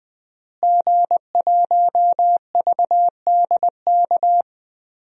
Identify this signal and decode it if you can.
Morse code, heard as a keyed tone, about 20 wpm, 700 Hz. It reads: G1VDK
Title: DXCC-100 Call Signs - 5 Characters - Repeated 3x - 20wpm